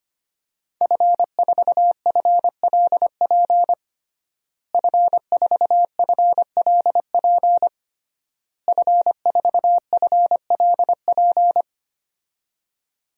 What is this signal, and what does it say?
Morse code, heard as a keyed tone, about 25 wpm, 700 Hz. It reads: F4FLP F4FLP F4FLP
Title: DXCC-100 Call Signs - 5 Characters - Repeated 3x - 25wpm